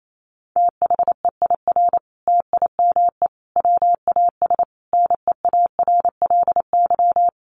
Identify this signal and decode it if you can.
Morse code, heard as a keyed tone, about 28 wpm, 700 Hz. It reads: THEIR TIME WAS NEARLY
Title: Sentences from Top 1300 Words 28wpm